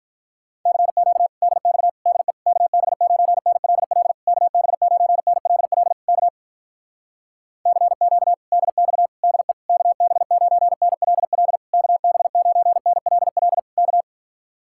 Morse code, heard as a keyed tone, about 46 wpm, 700 Hz. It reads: CQ DX DE KB9NLL KB9NLL K CQ DX DE KB9NLL KB9NLL K